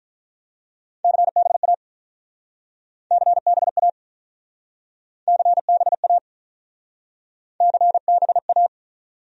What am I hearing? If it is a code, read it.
Morse code, heard as a keyed tone, about 45 wpm, 700 Hz. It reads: CBA CBA CBA CBA